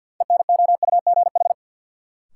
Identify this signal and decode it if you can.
Morse code, heard as a keyed tone, about 53 wpm, 700 Hz. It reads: ENOUGH
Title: Sets of 2 Words - New Word Drill - 50wpm